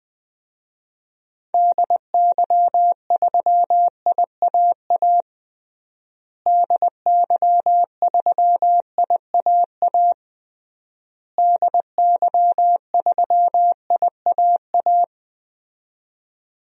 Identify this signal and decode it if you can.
Morse code, heard as a keyed tone, about 21 wpm, 700 Hz. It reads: DY3IAA DY3IAA DY3IAA
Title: DXCC-100 Call Signs - 6 Characters - Repeated 3x - 20wpm